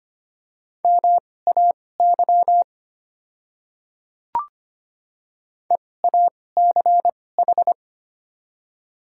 Morse code, heard as a keyed tone, about 25 wpm, spaced as for 18 wpm, 690 Hz. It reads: MAY EACH
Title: 2x Character Spacing - Top 100 Words - 25wpm